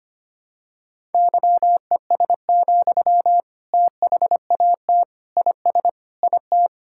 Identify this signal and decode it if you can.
Morse code, heard as a keyed tone, about 25 wpm, 700 Hz. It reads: YES, THAT IS IT
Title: Sentences from Top 1000 Words - Review Part 6 - 25wpm